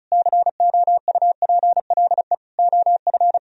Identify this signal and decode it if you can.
Morse code, heard as a keyed tone, about 46 wpm, 700 Hz. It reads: COUPLE OF